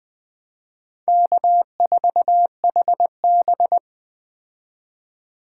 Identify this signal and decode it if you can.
Morse code, heard as a keyed tone, about 21 wpm, 700 Hz. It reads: K4HB